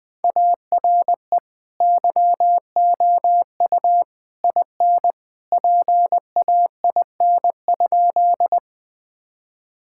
Morse code, heard as a keyed tone, about 20 wpm, 700 Hz. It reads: ARE YOU IN PAIN?
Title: Sentences from Top 1600 Words 20wpm